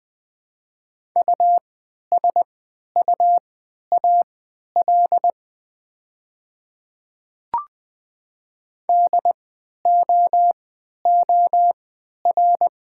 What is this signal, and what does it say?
Morse code, heard as a keyed tone, about 20 wpm, spaced as for 11 wpm, 700 Hz. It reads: USUAL DOOR
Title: 3x Character Spacing - Top 500 Words - 20wpm